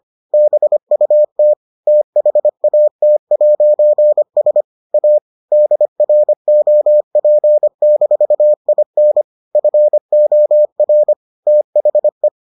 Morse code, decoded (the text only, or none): BUT THAT'S A DROP-IN FOR THE